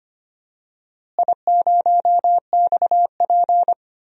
Morse code, heard as a keyed tone, about 25 wpm, 700 Hz. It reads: I0XP
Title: DXCC-100 Call Signs - 4 Characters - Repeated 3x - 25wpm